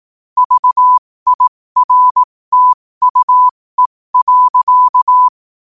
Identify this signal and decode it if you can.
Morse code, heard as a keyed tone, about 18 wpm, spaced as for 15 wpm, 1000 Hz. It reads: VIRTUE.
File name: qotdfiles / 15wpm